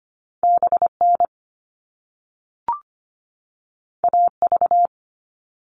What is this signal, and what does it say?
Morse code, heard as a keyed tone, about 25 wpm, 700 Hz. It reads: BN AV
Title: All Cut Numbers - 2-Digit 25wpm